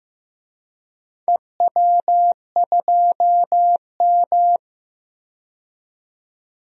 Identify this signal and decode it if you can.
Morse code, heard as a keyed tone, about 15 wpm, 700 Hz. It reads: EW2M
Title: DXCC-100 Call Signs - 4 Characters - 15wpm